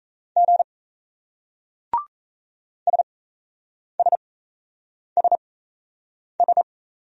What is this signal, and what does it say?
Morse code, heard as a keyed tone, about 38 wpm, spaced as for 8 wpm, 700 Hz. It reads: G SSSS